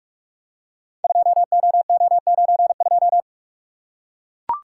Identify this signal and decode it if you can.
Morse code, heard as a keyed tone, about 44 wpm, 700 Hz. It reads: 2OO92
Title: Code Groups - 5 Letter:Numbers - 45wpm